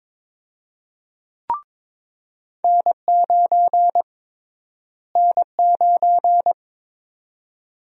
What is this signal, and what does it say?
Morse code, heard as a keyed tone, about 21 wpm, 700 Hz. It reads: N9 N9
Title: USA Call Sign Prefixes - Repeated 2x - 22wpm